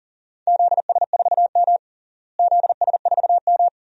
Morse code, heard as a keyed tone, about 40 wpm, 700 Hz. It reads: ZS4M ZS4M
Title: DXCC-100 Call Signs - 4 Characters - Repeated 2x - 40wpm